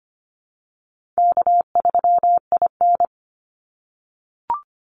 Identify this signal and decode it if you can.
Morse code, heard as a keyed tone, about 25 wpm, 700 Hz. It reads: K3IN
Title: Easy Call Signs - 25wpm